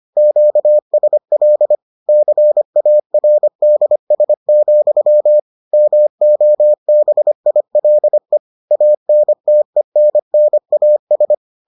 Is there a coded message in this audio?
Yes: Morse code from 0.2 to 11.3 s, heard as a keyed tone, about 25 wpm, 600 Hz. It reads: QSL CARDS, MOBILE ANTENNAS